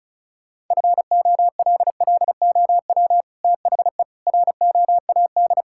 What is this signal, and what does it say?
Morse code, heard as a keyed tone, about 36 wpm, 700 Hz. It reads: FOLLOW THE ROAD
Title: Sentences from Top 1000 Words - Review Part 6 - 35wpm